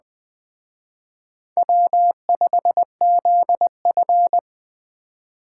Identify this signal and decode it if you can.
Morse code, heard as a keyed tone, about 20 wpm, 700 Hz. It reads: W5ZF